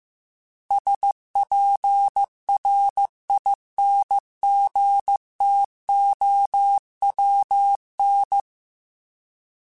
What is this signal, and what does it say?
Morse code, heard as a keyed tone, about 15 wpm, 800 Hz. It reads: SPRINGTOWN